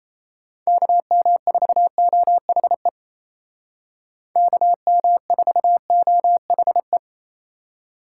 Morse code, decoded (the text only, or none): KM4OHE KM4OHE